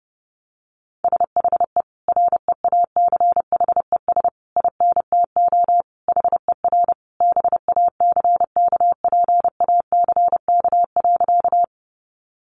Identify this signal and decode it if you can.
Morse code, heard as a keyed tone, about 28 wpm, 700 Hz. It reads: SHE REACHES INTO HER BACKPACK.
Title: Additional Easy Phrases 30wpm